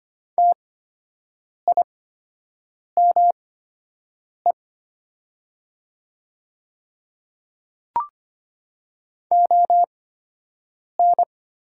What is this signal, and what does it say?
Morse code, heard as a keyed tone, about 25 wpm, spaced as for 7 wpm, 700 Hz. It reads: TIME ON
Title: 8x Character Spacing - Top 100 Words - 25wpm